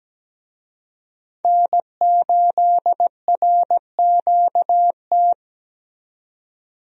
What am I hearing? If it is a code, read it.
Morse code, heard as a keyed tone, about 17 wpm, 700 Hz. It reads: N8RQT
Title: USA Call Signs - 5 Characters - 17wpm